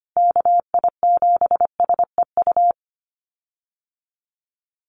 Morse code, heard as a keyed tone, about 25 wpm, 700 Hz. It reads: KI7SEU
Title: USA Call Signs Repeated 3x 25wpm